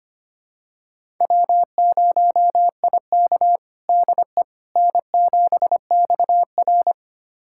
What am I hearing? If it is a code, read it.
Morse code, heard as a keyed tone, about 25 wpm, 700 Hz. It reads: W0IK DE N7XR
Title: USA Callsign DE Callsign - 4 Characters - 25wpm